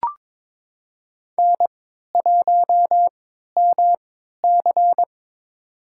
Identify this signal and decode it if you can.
Morse code, heard as a keyed tone, about 22 wpm, spaced as for 13 wpm, 700 Hz. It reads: N1MC
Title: USA Call Signs - 3x Character Spacing - 22wpm